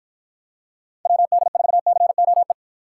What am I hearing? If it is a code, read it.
Morse code, heard as a keyed tone, about 48 wpm, 700 Hz. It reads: WD4COE